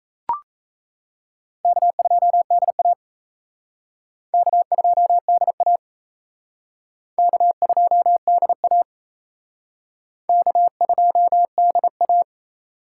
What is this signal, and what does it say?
Morse code, heard as a keyed tone, about 36 wpm, 700 Hz. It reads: K2DA K2DA K2DA K2DA